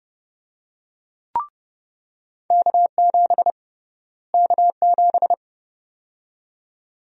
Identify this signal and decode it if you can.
Morse code, heard as a keyed tone, about 32 wpm, 700 Hz. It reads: K7 K7
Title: USA Call Sign Prefixes - Repeated 2x - 30wpm